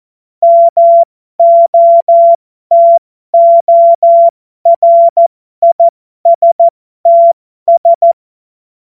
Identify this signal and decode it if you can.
Morse code, heard as a keyed tone, about 14 wpm, spaced as for 12 wpm, 680 Hz. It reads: MOTORISTS
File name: Morsekurs / Witze